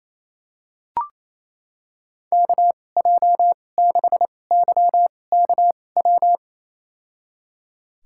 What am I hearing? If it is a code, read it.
Morse code, heard as a keyed tone, about 28 wpm, spaced as for 20 wpm, 700 Hz. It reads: KJ6YKW
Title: USA Call Signs - 2x Character Spacing - 28wpm